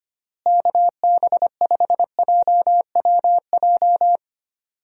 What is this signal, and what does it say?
Morse code, heard as a keyed tone, about 25 wpm, 700 Hz. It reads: KB5JWJ
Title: USA Call Signs - 6 Characters - Repeated 2x - 25wpm